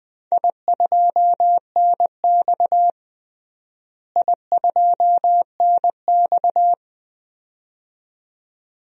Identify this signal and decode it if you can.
Morse code, heard as a keyed tone, about 21 wpm, 700 Hz. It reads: I2NX I2NX
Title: DXCC-100 Call Signs - 4 Characters - Repeated 2x - 20wpm